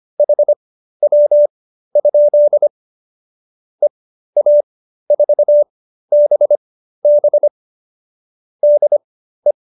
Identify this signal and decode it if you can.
Morse code, heard as a keyed tone, about 25 wpm, spaced as for 13 wpm, 590 Hz. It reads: HW? EA4BB DE